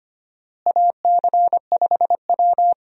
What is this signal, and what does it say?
Morse code, heard as a keyed tone, about 25 wpm, 700 Hz. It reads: AC5W
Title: USA Call Signs - 4 Characters - Repeated 2x - 25wpm